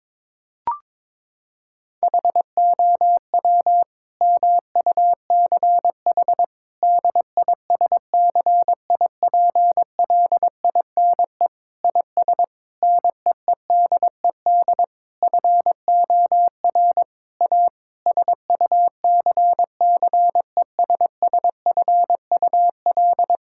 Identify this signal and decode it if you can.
Morse code, heard as a keyed tone, about 22 wpm, 700 Hz. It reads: HOW MUCH DISCIPLINE IS NEEDED FOR A SUCCESSFUL
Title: Sentences from Top 1000 Words - Review Part 6 - 22wpm